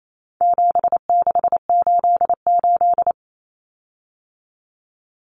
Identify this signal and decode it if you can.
Morse code, heard as a keyed tone, about 28 wpm, 700 Hz. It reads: 7688
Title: Common Cut Numbers - 4-Digit 28wpm